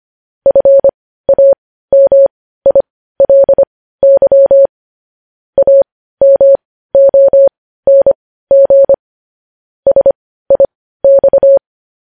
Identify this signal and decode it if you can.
Morse code, heard as a keyed tone, about 25 wpm, spaced as for 15 wpm, 550 Hz. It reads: FAMILY AMONG SIX
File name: podcast / News headlines-1x-25x15